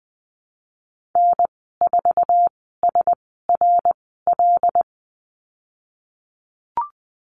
Morse code, heard as a keyed tone, about 20 wpm, spaced as for 15 wpm, 700 Hz. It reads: N4SRL
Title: 088 Lesson 15g - Number 5 - Call Signs 20wpm